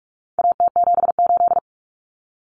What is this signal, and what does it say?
Morse code, heard as a keyed tone, about 44 wpm, 700 Hz. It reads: AT78
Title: Common Cut Numbers - 4-Digit 45wpm